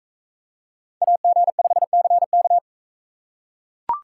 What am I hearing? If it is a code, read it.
Morse code, heard as a keyed tone, about 41 wpm, 700 Hz. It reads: AG5CK